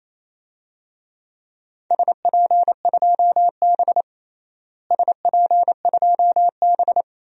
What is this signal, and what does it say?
Morse code, heard as a keyed tone, about 28 wpm, 700 Hz. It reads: SP2B SP2B
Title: DXCC-100 Call Signs - 4 Characters - Repeated 2x - 28wpm